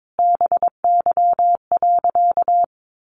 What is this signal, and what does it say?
Morse code, heard as a keyed tone, about 22 wpm, 700 Hz. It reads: BY.
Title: Sentences from Top 1500 Words 22wpm